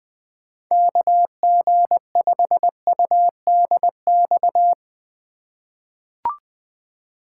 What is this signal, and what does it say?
Morse code, heard as a keyed tone, about 20 wpm, 700 Hz. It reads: KG5UDX
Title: USA Call Signs - 3x Character Spacing - 20wpm